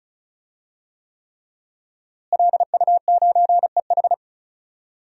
Morse code, heard as a keyed tone, about 34 wpm, 700 Hz. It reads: LU9EH
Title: DXCC-100 Call Signs - 5 Characters - Repeated 3x - 35wpm